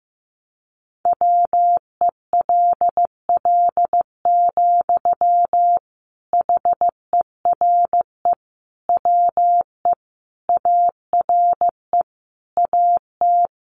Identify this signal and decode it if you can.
Morse code, heard as a keyed tone, about 15 wpm, 700 Hz. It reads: WELL, HERE WE ARE AT
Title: Sentences from Top 300 Words 15wpm